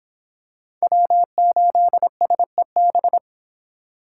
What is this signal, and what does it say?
Morse code, heard as a keyed tone, about 26 wpm, 700 Hz. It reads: W8SEB